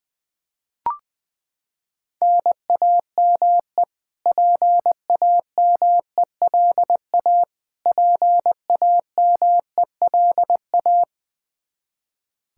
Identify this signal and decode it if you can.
Morse code, heard as a keyed tone, about 20 wpm, 700 Hz. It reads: NAME PAMELA PAMELA
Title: Practice QSO - Exchange 1 - Beginning - 20wpm